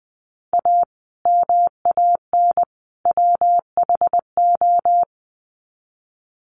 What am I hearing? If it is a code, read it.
Morse code, heard as a keyed tone, about 20 wpm, 700 Hz. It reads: A MAN WHO
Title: Sets of 3 Words 20wpm